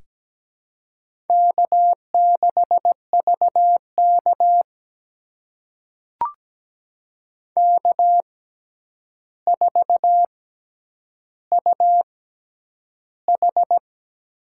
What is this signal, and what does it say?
Morse code, heard as a keyed tone, about 17 wpm, spaced as for 9 wpm, 700 Hz. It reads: K6VK K4UH
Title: USA Call Signs - 4 Characters - 6x Character Spacing - 17wpm